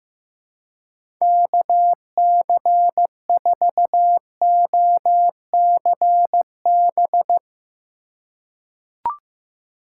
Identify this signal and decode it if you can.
Morse code, heard as a keyed tone, about 15 wpm, 700 Hz. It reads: KC4OCB
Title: USA Call Signs - 2x Character Spacing - 15wpm